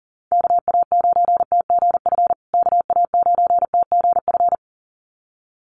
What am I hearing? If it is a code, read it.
Morse code, heard as a keyed tone, about 40 wpm, 700 Hz. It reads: KA9TGF KA9TGF
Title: USA Call Signs Repeated 2x 40wpm